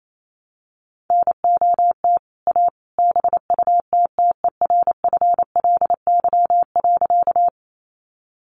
Morse code, heard as a keyed tone, about 28 wpm, 700 Hz. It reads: NOT A BUTTERFLY.
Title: A Sound Of Thunder 28wpm